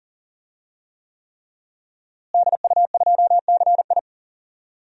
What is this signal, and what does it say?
Morse code, heard as a keyed tone, about 40 wpm, 700 Hz. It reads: DU2CI
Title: DXCC-100 Call Signs - 5 Characters - Repeated 3x - 40wpm